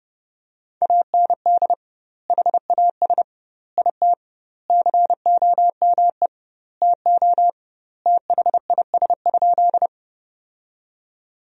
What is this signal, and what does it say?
Morse code, heard as a keyed tone, about 30 wpm, 700 Hz. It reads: AND HAS IT COME TO THIS?
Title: 2x Word-Spacing - Sentences from Top 100 Words - 30wpm